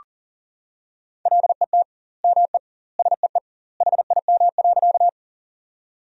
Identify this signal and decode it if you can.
Morse code, heard as a keyed tone, about 40 wpm, 700 Hz. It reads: LET ME SEE HIM.
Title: 2x Word-Spacing - Sentences from Top 300 Words - 40wpm